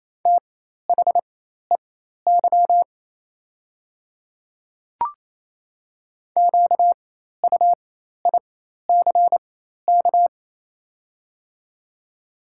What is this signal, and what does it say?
Morse code, heard as a keyed tone, about 28 wpm, spaced as for 13 wpm, 690 Hz. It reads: THEY QUICK
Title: 4x Character Spacing - Top 500 Words - 28wpm